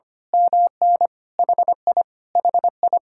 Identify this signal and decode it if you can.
Morse code, heard as a keyed tone, about 25 wpm, 700 Hz. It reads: MN HI HI
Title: Speed-Racing Common QSO Phrases Encore - 25wpm